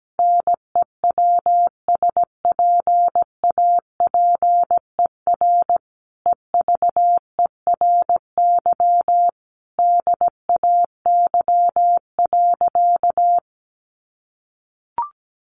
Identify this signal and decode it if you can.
Morse code, heard as a keyed tone, about 17 wpm, 700 Hz. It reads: NEWSPAPER EVERY DAY.
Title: Sentences from Top 1100 Words 17wpm